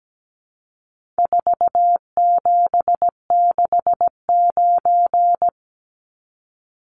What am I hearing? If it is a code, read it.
Morse code, heard as a keyed tone, about 17 wpm, 700 Hz. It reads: 4769
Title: Numbers - 4-Digit 17wpm